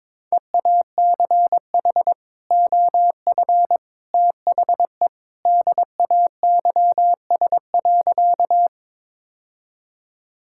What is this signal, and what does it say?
Morse code, heard as a keyed tone, about 22 wpm, 700 Hz. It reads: EACH OF THE DAYS.
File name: practice / 22wpm / Sentences with 3-Word Phrases - 22wpm